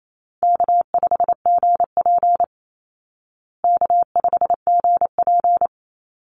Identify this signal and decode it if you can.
Morse code, heard as a keyed tone, about 28 wpm, 700 Hz. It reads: K5GP K5GP
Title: Call Signs Repeated 3x - 28wpm